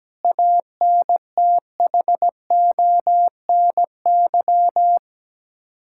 Morse code, heard as a keyed tone, about 17 wpm, 700 Hz. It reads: ANTHONY
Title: Practice QSO - Exchange 1 - Beginning - 17wpm